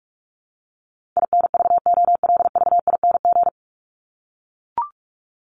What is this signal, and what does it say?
Morse code, heard as a keyed tone, about 44 wpm, 700 Hz. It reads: INVOLVING
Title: ING Suffix - Top 100 Words with Base Words 45wpm